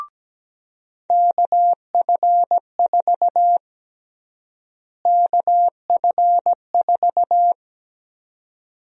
Morse code, heard as a keyed tone, about 17 wpm, 700 Hz. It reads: KF4 KF4
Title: USA Call Sign Prefixes - Repeated 2x - 17wpm